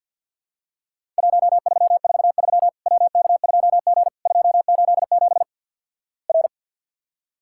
Morse code, heard as a keyed tone, about 52 wpm, 700 Hz. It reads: 1343 WK2G 287 R
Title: Worked All Europe DX Contest - QTC Exchange - 50wpm